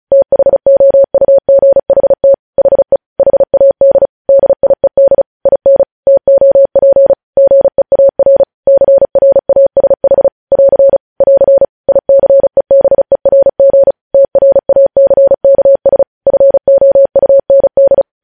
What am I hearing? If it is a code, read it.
Morse code, heard as a keyed tone, about 36 wpm, 550 Hz. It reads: THOUGHT HE HAD DIED IN TOP GEAR CRASH <AR> <AR> ICEBERG TRACKS FOUND
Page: news headlines in morse code at 35 wpm